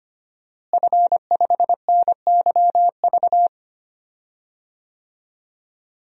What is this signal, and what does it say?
Morse code, heard as a keyed tone, about 25 wpm, 700 Hz. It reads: F5NYV
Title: DXCC-100 Call Signs - 5 Characters - Repeated 3x - 25wpm